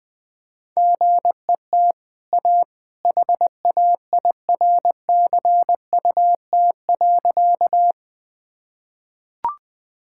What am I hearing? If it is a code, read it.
Morse code, heard as a keyed tone, about 20 wpm, 700 Hz. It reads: GET A HAIRCUT.
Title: Easy Sentences with Hints - 20wpm